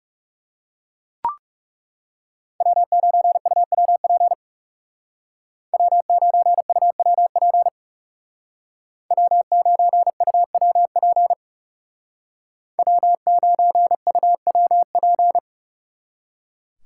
Morse code, heard as a keyed tone, about 35 wpm, 700 Hz. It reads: W9UWP W9UWP W9UWP W9UWP
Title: Speed Racing - USA 1x3 Call Signs 30wpm